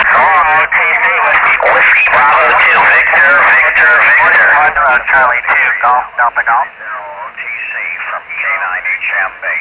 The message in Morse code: none